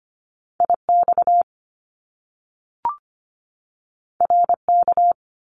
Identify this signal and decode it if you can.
Morse code, heard as a keyed tone, about 25 wpm, 700 Hz. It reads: IX RK
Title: Top Two Letter Combinations in Words 25wpm